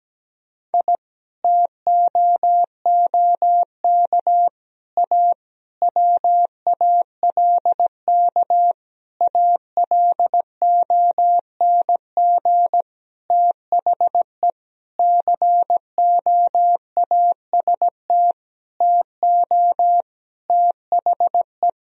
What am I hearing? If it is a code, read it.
Morse code, heard as a keyed tone, about 17 wpm, 700 Hz. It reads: I TOOK A WALK ALONG THE COAST TO THE
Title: Sentences from Top 1000 Words - Review Part 4 - 17wpm